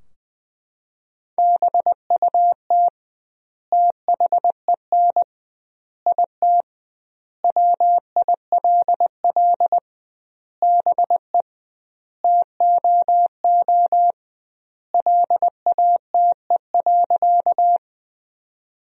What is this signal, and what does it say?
Morse code, heard as a keyed tone, about 20 wpm, 700 Hz. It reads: BUT THEN IT WILL BE TOO LATE.
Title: 2x Word-Spacing - Sentences from Top 300 Words - 20wpm